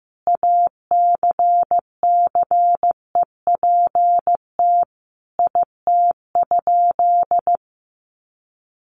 Morse code, heard as a keyed tone, about 15 wpm, 700 Hz. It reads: ACCEPT IT?
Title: Sentences from Top 1300 Words 15wpm